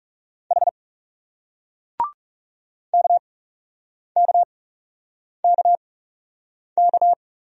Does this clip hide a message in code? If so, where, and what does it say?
Yes: Morse code from 0.5 to 7.1 s, heard as a keyed tone, about 41 wpm, 700 Hz. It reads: H K K K K